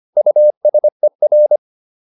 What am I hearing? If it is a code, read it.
Morse code, heard as a keyed tone, about 25 wpm, 600 Hz. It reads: USER